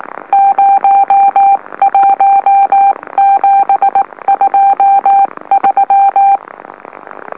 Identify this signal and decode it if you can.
Morse code, heard as a keyed tone, about 18 wpm, 790 Hz. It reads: 01723
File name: M04- -2345utc-LOLO